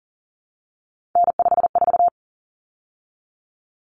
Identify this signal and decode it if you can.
Morse code, heard as a keyed tone, about 40 wpm, 700 Hz. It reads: N54
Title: Common Cut Numbers - 3-Digit 40wpm